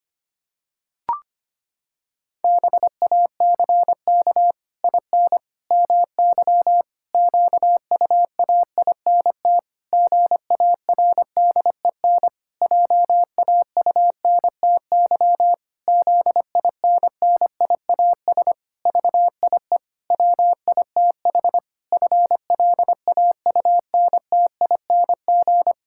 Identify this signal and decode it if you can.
Morse code, heard as a keyed tone, about 25 wpm, 700 Hz. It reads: BACK IN MY QUAINT GARDEN JAUNTY ZINNIAS VIE WITH FLAUNTING
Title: Pangram Challenge v2 - 25wpm